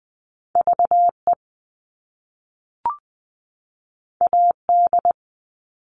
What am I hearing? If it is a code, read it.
Morse code, heard as a keyed tone, about 20 wpm, 700 Hz. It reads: VE AD
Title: All Cut Numbers - 2-Digit 20wpm